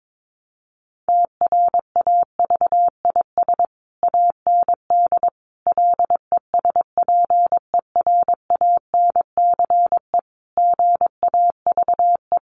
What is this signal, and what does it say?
Morse code, heard as a keyed tone, about 22 wpm, 700 Hz. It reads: TRAVIS AND LESPERANCE GAVE
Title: A Sound Of Thunder 22wpm